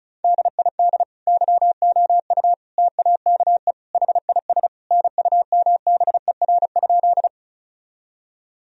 Morse code, heard as a keyed tone, about 36 wpm, 700 Hz. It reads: DID YOU TAKE HIS NUMBER?
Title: Sentences from Top 1000 Words - Review Part 1 - 35wpm